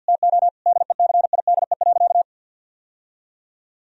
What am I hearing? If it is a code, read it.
Morse code, heard as a keyed tone, about 50 wpm, 700 Hz. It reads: TO DECIDE.
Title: Sentences from Top 1000 Words - Review Part 3 - 50wpm